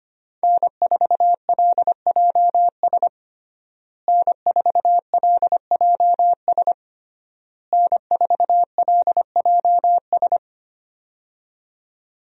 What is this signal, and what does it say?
Morse code, heard as a keyed tone, about 25 wpm, 700 Hz. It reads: N4LJS N4LJS N4LJS
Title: USA Call Signs - 5 Characters - Repeated 3x - 25wpm